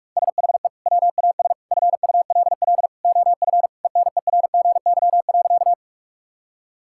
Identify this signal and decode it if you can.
Morse code, heard as a keyed tone, about 46 wpm, 700 Hz. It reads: SHE WAS FULL OF ENERGY.
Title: Sentences from Top 1000 Words - Review Part 6 - 45wpm